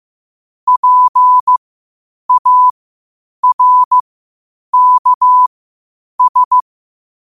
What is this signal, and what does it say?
Morse code, heard as a keyed tone, about 15 wpm, spaced as for 8 wpm, 1000 Hz. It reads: PARKS